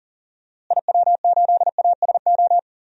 Morse code, heard as a keyed tone, about 40 wpm, 700 Hz. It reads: IW8ASO
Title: DXCC-100 Call Signs - 6 Characters - Repeated 2x - 40wpm